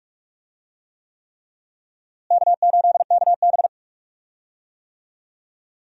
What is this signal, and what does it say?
Morse code, heard as a keyed tone, about 43 wpm, 700 Hz. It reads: K8KB
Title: USA Call Signs - 4 Characters - Repeated 3x - 45wpm